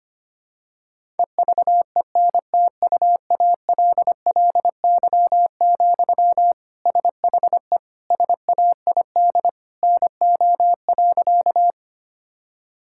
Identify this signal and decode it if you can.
Morse code, heard as a keyed tone, about 25 wpm, 700 Hz. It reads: EVENTUALLY, SHE SAID NO.
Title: Sentences from Top 1000 Words - Review Part 6 - 25wpm